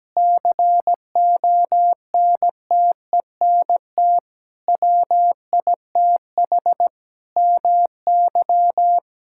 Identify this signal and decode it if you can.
Morse code, heard as a keyed tone, about 17 wpm, 700 Hz. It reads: CONTENT WITH MY
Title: Sentences from Top 1000 Words - Review Part 6 - 17wpm